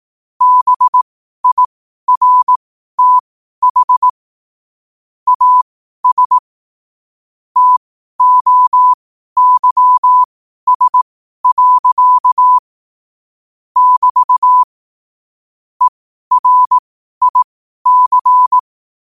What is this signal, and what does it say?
Morse code, heard as a keyed tone, about 18 wpm, spaced as for 12 wpm, 1000 Hz. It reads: BIRTH AS TOYS. <BT> ERIC
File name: qotdfiles / 12wpm